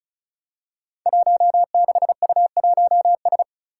Morse code, heard as a keyed tone, about 36 wpm, 700 Hz. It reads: 16U1S
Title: Five Challenging Letter-Number Combinations 35wpm